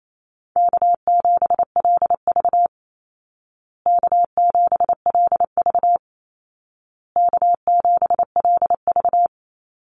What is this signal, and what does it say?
Morse code, heard as a keyed tone, about 28 wpm, 700 Hz. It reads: K7LV K7LV K7LV
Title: Call Signs Repeated 2x - 28wpm